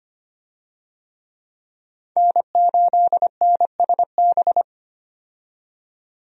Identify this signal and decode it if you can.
Morse code, heard as a keyed tone, about 25 wpm, 700 Hz. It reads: N8NSB
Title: USA Call Signs - 5 Characters - 25wpm